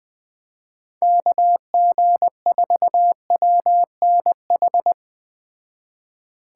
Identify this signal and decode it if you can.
Morse code, heard as a keyed tone, about 20 wpm, 700 Hz. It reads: KG4WNH
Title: Speed Racing - USA 2x3 Call Signs 17wpm